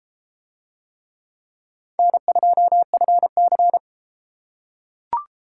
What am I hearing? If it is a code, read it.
Morse code, heard as a keyed tone, about 30 wpm, 700 Hz. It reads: N2FC